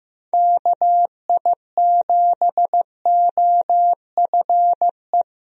Morse code, heard as a keyed tone, about 15 wpm, 700 Hz. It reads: KI7OFE
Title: USA Call Signs - 6 Characters - Repeated 3x - 15wpm